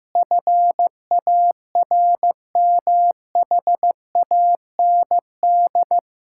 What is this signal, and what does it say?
Morse code, heard as a keyed tone, about 15 wpm, 700 Hz. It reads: FARMHAND
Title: Pangram Challenge v2 - 15wpm